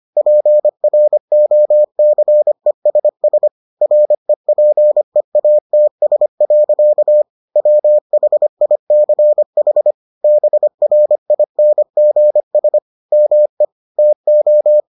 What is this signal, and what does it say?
Morse code, heard as a keyed tone, about 25 wpm, 600 Hz. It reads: PROCESS REPEATS. WHICH BRINGS ME TO